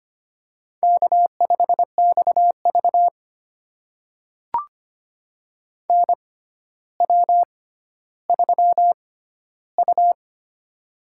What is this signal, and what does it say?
Morse code, heard as a keyed tone, about 25 wpm, spaced as for 13 wpm, 700 Hz. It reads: K5XV NW3U
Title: USA Call Signs - 4 Characters - 6x Character Spacing - 25wpm